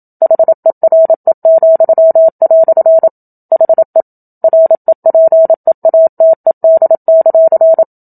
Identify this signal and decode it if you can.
Morse code, heard as a keyed tone, about 27 wpm, 650 Hz. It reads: HERE," HE REPEATED;
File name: Book Two / 01-Chapter 1 - 27 WPM